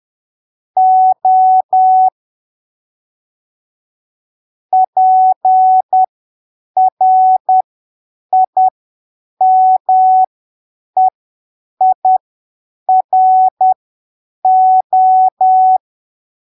Morse code, heard as a keyed tone, about 10 wpm, spaced as for 7 wpm, 750 Hz. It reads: O PRIMEIRO
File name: arquivos / texto / texto 16 10ppm